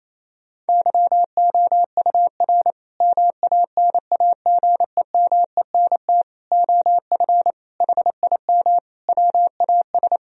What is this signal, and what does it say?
Morse code, heard as a keyed tone, about 28 wpm, 700 Hz. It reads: YOUR MANAGEMENT OF HIM WAS
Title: Sentences from Top 1000 Words - Review Part 4 - 28wpm